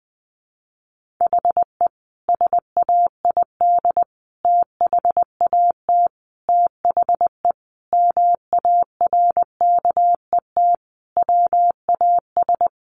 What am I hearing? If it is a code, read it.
Morse code, heard as a keyed tone, about 20 wpm, 700 Hz. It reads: HE SAID THAT THE MARKET WAS
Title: Sentences from Top 1400 Words 20wpm